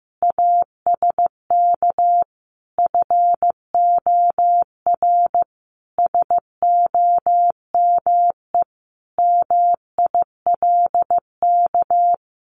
Morse code, heard as a keyed tone, about 15 wpm, 700 Hz. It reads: ASK FOR SOME MILK
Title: Sentences from Top 1500 Words 15wpm